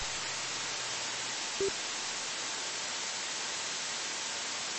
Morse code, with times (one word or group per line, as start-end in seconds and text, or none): none